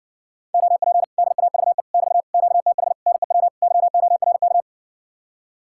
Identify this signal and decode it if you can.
Morse code, heard as a keyed tone, about 60 wpm, 700 Hz. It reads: OP DAVE <BT> QTH NEW YORK